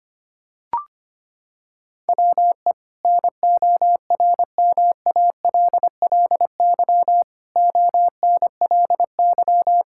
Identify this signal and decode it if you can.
Morse code, heard as a keyed tone, about 25 wpm, 700 Hz. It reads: WE NORMALLY ONLY
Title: Sentences from Top 1000 Words - Review Part 6 - 25wpm